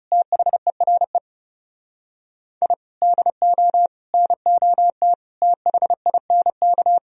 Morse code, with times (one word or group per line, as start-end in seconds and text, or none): 0.1-1.2 THERE
2.6-2.7 I
3.0-3.9 DO
4.1-5.1 NOT
5.4-7.0 THINK